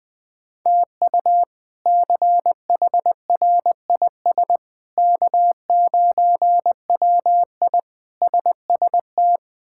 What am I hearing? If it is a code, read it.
Morse code, heard as a keyed tone, about 20 wpm, 700 Hz. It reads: TU CHRIS K9WI SST
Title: K1USN SST QSO Practice - 20wpm